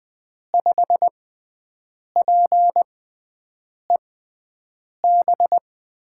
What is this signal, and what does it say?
Morse code, heard as a keyed tone, about 20 wpm, spaced as for 7 wpm, 700 Hz. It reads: 5PEB
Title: USA Call Signs - 6 Characters - 6x Character Spacing - 20wpm